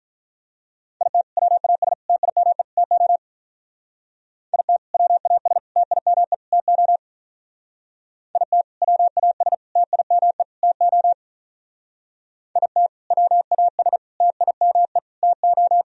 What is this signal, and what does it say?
Morse code, heard as a keyed tone, about 43 wpm, 700 Hz. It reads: IT WAS TIME TO IT WAS TIME TO IT WAS TIME TO IT WAS TIME TO
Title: Speed-Racing - Sets of 4 Words 35wpm